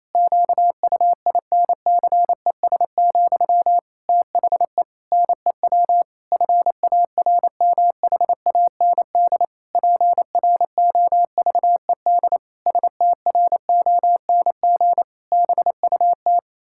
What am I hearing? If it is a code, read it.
Morse code, heard as a keyed tone, about 28 wpm, 700 Hz. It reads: QUINCES, THE NEW FARMHAND PROVED STRONG BUT
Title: Pangram Challenge v2 - 28wpm